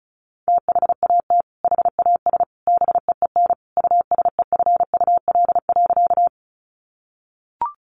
Morse code, heard as a keyed tone, about 34 wpm, 700 Hz. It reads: THAT HAS BEEN USEFUL.
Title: Sentences from Top 700 Words 35wpm